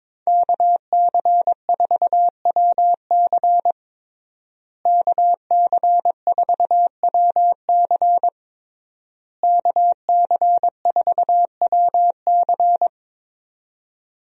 Morse code, heard as a keyed tone, about 22 wpm, 700 Hz. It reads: KC4WC KC4WC KC4WC
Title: USA Call Signs - 5 Characters - Repeated 3x - 22wpm